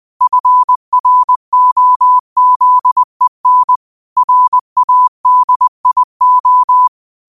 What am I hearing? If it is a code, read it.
Morse code, heard as a keyed tone, about 20 wpm, 1000 Hz. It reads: FROZEN RADIO
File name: 20wpm